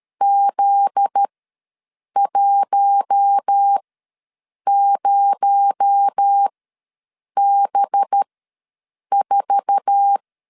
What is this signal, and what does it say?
Morse code, heard as a keyed tone, about 13 wpm, spaced as for 7 wpm, 800 Hz. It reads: Z10B4
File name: LEZIONI / 9 2867 ws 12wpm 3431